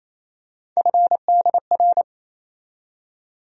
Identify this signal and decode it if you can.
Morse code, heard as a keyed tone, about 28 wpm, 700 Hz. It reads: FDR
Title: USA Call Sign Postfixes - 3 Letters 28wpm